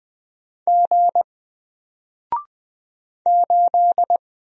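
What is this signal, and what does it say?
Morse code, heard as a keyed tone, about 20 wpm, 700 Hz. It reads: G 8
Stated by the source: Single letter and number Morse drill at 20 wpm, from N4LCD.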